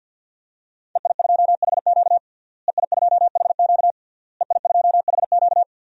Morse code, heard as a keyed tone, about 52 wpm, 700 Hz. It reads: EI2HQ EI2HQ EI2HQ
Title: DXCC-100 Call Signs - 5 Characters - Repeated 3x - 50wpm